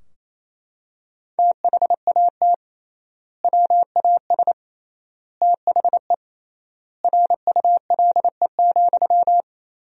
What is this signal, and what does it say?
Morse code, heard as a keyed tone, about 28 wpm, 700 Hz. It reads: THAT WAS THE RULE,
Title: 3x Word-Spacing - Sentences from Top 400 Words - 28wpm